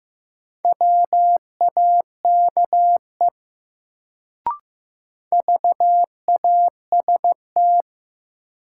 Morse code, heard as a keyed tone, about 15 wpm, 700 Hz. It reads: WAKE VAST
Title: Sets of 4 Words - New Word Drill - 15wpm